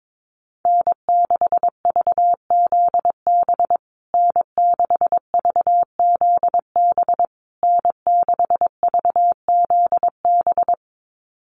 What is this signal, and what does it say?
Morse code, heard as a keyed tone, about 22 wpm, 690 Hz. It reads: N6VZB N6VZB N6VZB
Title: USA Call Signs Repeated 3x 22wpm